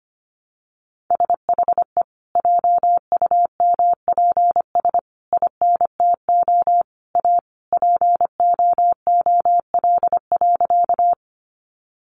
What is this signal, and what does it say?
Morse code, heard as a keyed tone, about 25 wpm, 700 Hz. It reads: SHE JUMPS INTO A POOL.
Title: Additional Easy Phrases 25wpm